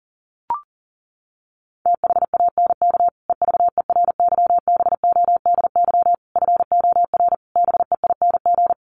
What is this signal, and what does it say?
Morse code, heard as a keyed tone, about 40 wpm, 700 Hz. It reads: THANK EVERYBODY FOR BEING
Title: Sentences from Top 1400 Words 40wpm